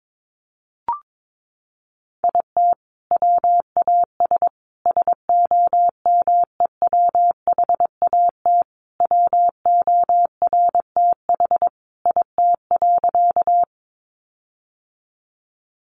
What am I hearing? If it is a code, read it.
Morse code, heard as a keyed tone, about 22 wpm, 700 Hz. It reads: IT WAS SOMEWHAT WORTH IT.
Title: Sentences from Top 1300 Words 22wpm